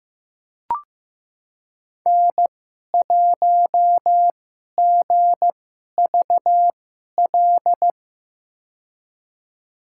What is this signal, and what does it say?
Morse code, heard as a keyed tone, about 15 wpm, spaced as for 11 wpm, 700 Hz. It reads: N1GVL
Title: USA Call Signs - 5 Characters - 2x Character Spacing - 15wpm